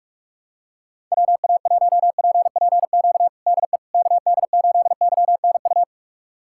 Morse code, heard as a keyed tone, about 46 wpm, 700 Hz. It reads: WA1PPQ DE KD8YNU
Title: USA Callsign DE Callsign - 6 Characters - 45wpm